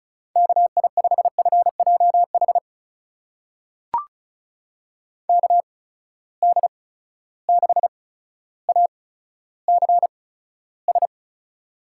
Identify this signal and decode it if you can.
Morse code, heard as a keyed tone, about 36 wpm, spaced as for 15 wpm, 700 Hz. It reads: KI5FJH KD6ACS
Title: USA Call Signs - 6 Characters - 8x Character Spacing - 35wpm